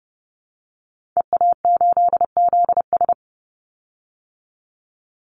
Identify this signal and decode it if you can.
Morse code, heard as a keyed tone, about 30 wpm, 700 Hz. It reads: EA8ZS
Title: Call Signs - 30wpm